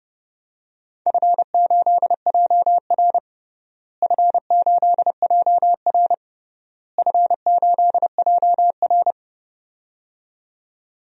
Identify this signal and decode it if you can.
Morse code, heard as a keyed tone, about 30 wpm, 700 Hz. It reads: F8JR F8JR F8JR
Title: DXCC-100 Call Signs - 4 Characters - Repeated 3x - 30wpm